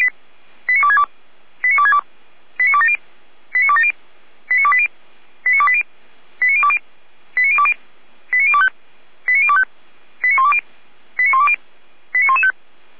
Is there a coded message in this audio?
Yes: five-tone selective call (ZVEI) from 0.7 to 12.5 s, heard as quick runs of five tones, tones of about 70 ms. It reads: 89262 89262 89270 89270 89280 89280 89020 89020 89025 89025 89120 89120 89115